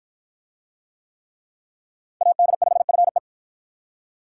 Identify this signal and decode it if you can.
Morse code, heard as a keyed tone, about 47 wpm, 700 Hz. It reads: AD5FE